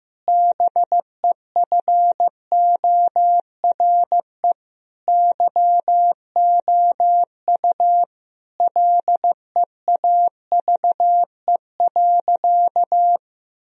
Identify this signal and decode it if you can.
Morse code, heard as a keyed tone, about 15 wpm, 700 Hz. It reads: BEFORE YOU LEAVE.
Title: Sentences from Top 1000 Words - Review Part 4 - 15wpm